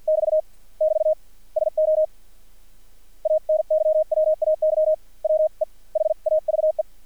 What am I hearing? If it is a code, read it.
Morse code, heard as a keyed tone, about 50 wpm, spaced as for 44 wpm, 620 Hz. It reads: <BT> <BT> SO ANYWAY WE HAVE
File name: archived / CRSnet